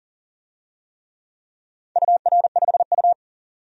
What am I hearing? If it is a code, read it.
Morse code, heard as a keyed tone, about 38 wpm, 700 Hz. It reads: UR5U
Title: DXCC-100 Call Signs - 4 Characters - 40wpm